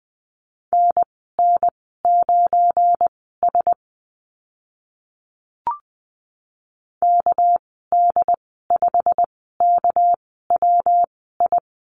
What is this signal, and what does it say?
Morse code, heard as a keyed tone, about 20 wpm, spaced as for 15 wpm, 700 Hz. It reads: NN9S KD5KWI
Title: 213 Lesson 34g - Number 8 - Call Signs 20wpm